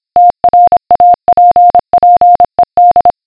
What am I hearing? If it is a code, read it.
Morse code, heard as a keyed tone, about 26 wpm, 710 Hz. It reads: TRAPPED